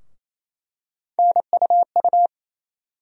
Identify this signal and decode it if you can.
Morse code, heard as a keyed tone, about 28 wpm, 700 Hz. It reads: NUU
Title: USA Call Sign Postfixes - 3 Letters 28wpm